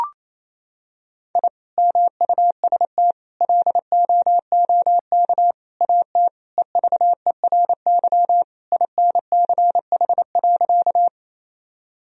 Morse code, heard as a keyed tone, about 28 wpm, 700 Hz. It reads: I MUST LOOK AT EVERY INCH.
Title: Sentences from Top 1000 Words - Review Part 3 - 28wpm